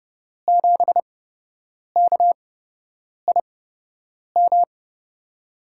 Morse code, heard as a keyed tone, about 29 wpm, spaced as for 8 wpm, 700 Hz. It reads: 7KIM
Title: USA Call Signs - 5 Characters - 8x Character Spacing - 30wpm